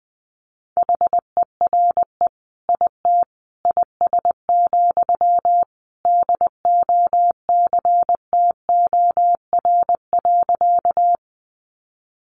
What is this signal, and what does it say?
Morse code, heard as a keyed tone, about 20 wpm, 700 Hz. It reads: HERE IT IS, DOCTOR.
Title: Sentences from Top 1500 Words 20wpm